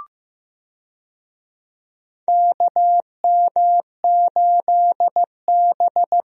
Morse code, heard as a keyed tone, about 15 wpm, 700 Hz. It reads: KM8B
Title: USA Call Signs - 4 Characters - Repeated 2x - 15wpm